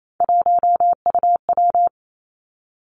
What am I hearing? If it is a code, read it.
Morse code, heard as a keyed tone, about 28 wpm, 700 Hz. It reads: 1UW
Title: Three Challenging Letter-Number Combinations 28wpm